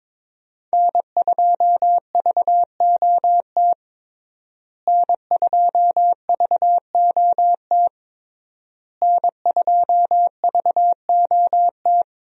Morse code, heard as a keyed tone, about 21 wpm, 700 Hz. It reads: N2VOT N2VOT N2VOT
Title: USA Call Signs - 5 Characters - Repeated 3x - 22wpm